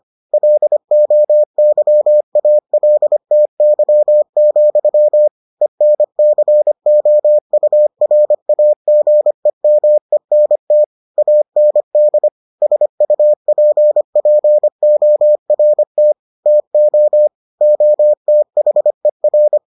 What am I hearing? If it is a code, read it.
Morse code, heard as a keyed tone, about 25 wpm, 600 Hz. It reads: LOYALTY, ENCOURAGEMENT AND SUPPORT TO OTHER